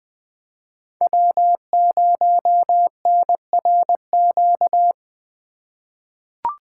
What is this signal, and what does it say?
Morse code, heard as a keyed tone, about 20 wpm, 700 Hz. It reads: W0NRQ
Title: USA Call Signs - 5 Characters - 8x Character Spacing - 20wpm